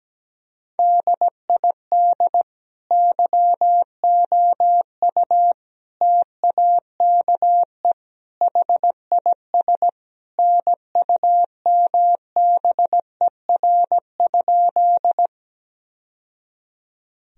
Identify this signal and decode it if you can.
Morse code, heard as a keyed tone, about 17 wpm, 700 Hz. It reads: DID YOU TAKE HIS NUMBER?
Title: Sentences from Top 1000 Words - Review Part 1 - 17wpm